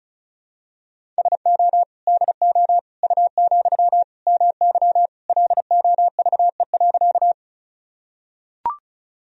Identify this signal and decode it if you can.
Morse code, heard as a keyed tone, about 36 wpm, 700 Hz. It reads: SO DO U, MY LOVE.
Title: Sentences from Top 400 Words and Abbreviations 35wpm